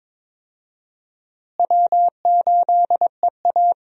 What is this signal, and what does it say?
Morse code, heard as a keyed tone, about 21 wpm, 700 Hz. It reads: W8EA
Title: USA Call Signs - 4 Characters - Repeated 3x - 22wpm